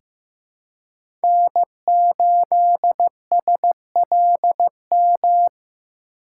Morse code, heard as a keyed tone, about 15 wpm, 700 Hz. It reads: N8SLM
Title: USA Call Signs - 5 Characters - 3x Character Spacing - 15wpm